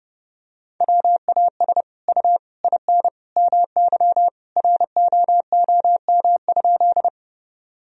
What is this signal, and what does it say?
Morse code, heard as a keyed tone, about 29 wpm, 700 Hz. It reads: WAS U IN MY ROOM?
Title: Sentences from Top 300 Words and Abbreviations 30wpm